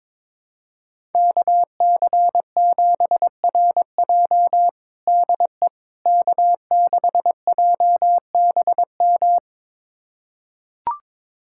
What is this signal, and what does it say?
Morse code, heard as a keyed tone, about 21 wpm, 700 Hz. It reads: KC7RJ DE K6JBM
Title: USA Callsign DE Callsign - 5 Characters - 22wpm